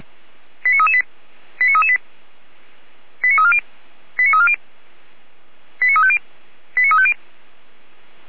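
Five-tone selective call (ZVEI), heard as quick runs of five tones, tones of about 70 ms. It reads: 89338 89338 89340 89340 89350 89350
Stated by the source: ZVEI five-tone call received on SDR, from audio.f85360.feuerwehr-orlamuende.org, a fire brigade in Orlamünde.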